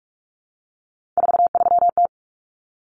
Morse code, heard as a keyed tone, about 44 wpm, 700 Hz. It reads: 43T
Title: Common Cut Numbers - 3-Digit 45wpm